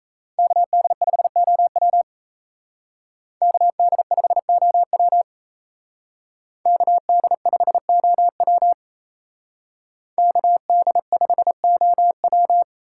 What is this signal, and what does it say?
Morse code, heard as a keyed tone, about 34 wpm, 700 Hz. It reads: KD5OW KD5OW KD5OW KD5OW
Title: Speed Racing - USA 2x2 Call Signs 28wpm